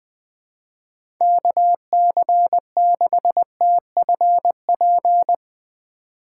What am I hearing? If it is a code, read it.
Morse code, heard as a keyed tone, about 20 wpm, 700 Hz. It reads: KC6TFP
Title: USA Call Signs - 3x Character Spacing - 20wpm